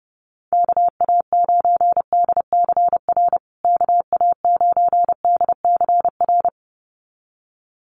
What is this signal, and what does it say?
Morse code, heard as a keyed tone, about 30 wpm, 700 Hz. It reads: KA9DCR KA9DCR
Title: USA Call Signs Repeated 2x 30wpm